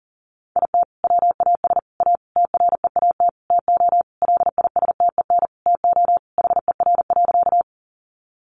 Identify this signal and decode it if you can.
Morse code, heard as a keyed tone, about 40 wpm, 700 Hz. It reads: IT WAS A TREAT TO LISTEN TO HER.
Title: Sentences from Top 1300 Words 40wpm